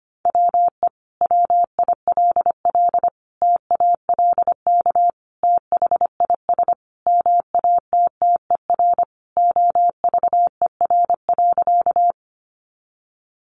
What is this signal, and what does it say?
Morse code, heard as a keyed tone, about 25 wpm, 700 Hz. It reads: WE WILL TALK THIS MATTER OVER.
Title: Sentences from Top 700 Words 25wpm